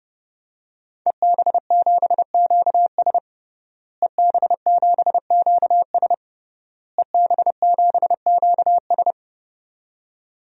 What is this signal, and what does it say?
Morse code, heard as a keyed tone, about 30 wpm, 700 Hz. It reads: EB7QS EB7QS EB7QS
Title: DXCC-100 Call Signs - 5 Characters - Repeated 3x - 30wpm